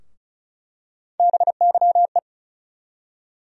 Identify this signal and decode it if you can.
Morse code, heard as a keyed tone, about 36 wpm, 700 Hz. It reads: BYE
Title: USA Call Sign Postfixes - 3 Letters 35wpm